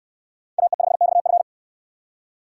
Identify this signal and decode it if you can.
Morse code, heard as a keyed tone, about 66 wpm, 700 Hz. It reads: AE5XF